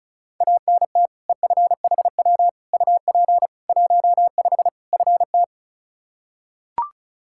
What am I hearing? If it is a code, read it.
Morse code, heard as a keyed tone, about 36 wpm, 700 Hz. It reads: ANT EFHW UP 15 FT
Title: Practice QSO - Exchange 2 - Info - 35wpm